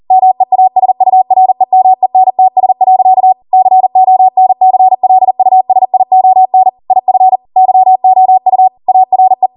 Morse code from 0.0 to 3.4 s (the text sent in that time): MEASUREMENTS.